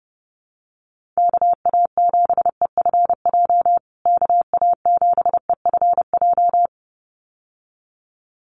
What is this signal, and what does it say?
Morse code, heard as a keyed tone, about 30 wpm, 700 Hz. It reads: KA7EFJ KA7EFJ
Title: USA Call Signs Repeated 2x 30wpm